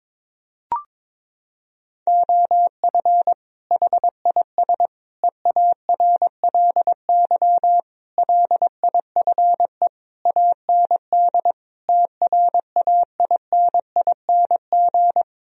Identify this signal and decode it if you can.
Morse code, heard as a keyed tone, about 22 wpm, 700 Hz. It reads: OF HIS EARLY LIFE AND TRAINING
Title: Sentences from Top 1000 Words - Review Part 4 - 22wpm